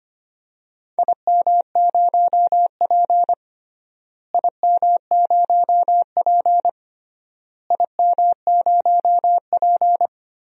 Morse code, heard as a keyed tone, about 25 wpm, 700 Hz. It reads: IM0P IM0P IM0P
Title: DXCC-100 Call Signs - 4 Characters - Repeated 3x - 25wpm